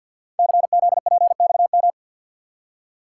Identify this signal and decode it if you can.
Morse code, heard as a keyed tone, about 52 wpm, 690 Hz. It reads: CZPXM
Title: Code Groups - 5 Letters - 50wpm